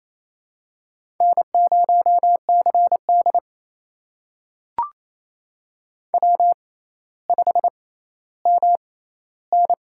Morse code, heard as a keyed tone, about 28 wpm, spaced as for 14 wpm, 700 Hz. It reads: N0CD W5MN
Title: USA Call Signs - 6x Character Spacing - 28wpm